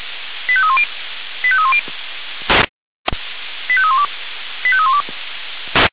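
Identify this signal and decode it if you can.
Five-tone selective call (ZVEI), heard as quick runs of five tones, tones of about 70 ms. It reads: 85310 85310 85312 85312
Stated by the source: Five-tone ZVEI selective call received on an SDR, logged by fire brigade Feuerwehr Orlamünde.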